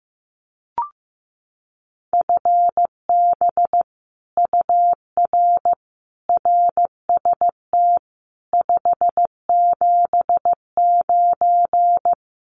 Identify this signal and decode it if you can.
Morse code, heard as a keyed tone, about 15 wpm, 700 Hz. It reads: FB UR RST 579
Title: Common QSO Phrases Encore 15wpm